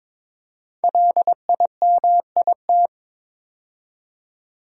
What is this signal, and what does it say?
Morse code, heard as a keyed tone, about 22 wpm, 700 Hz. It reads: LIMIT
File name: 5 Letter Words - First 3 Letters Given - 22wpm